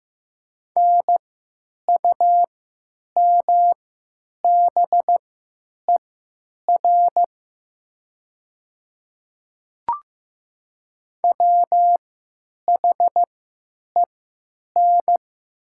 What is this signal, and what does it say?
Morse code, heard as a keyed tone, about 15 wpm, spaced as for 9 wpm, 700 Hz. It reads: NUMBER WHEN
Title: 3x Character Spacing - Top 100 Words - 15wpm